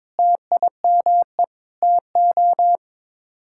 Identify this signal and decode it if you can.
Morse code, heard as a keyed tone, about 22 wpm, 700 Hz. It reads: TIME TO
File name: practice / 22wpm / Sets of 7 Words - 22wpm